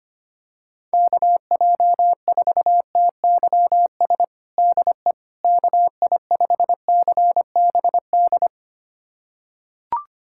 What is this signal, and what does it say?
Morse code, heard as a keyed tone, about 25 wpm, 700 Hz. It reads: KJ4TYS DE KI5CBD
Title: USA Callsign DE Callsign - 6 Characters - 25wpm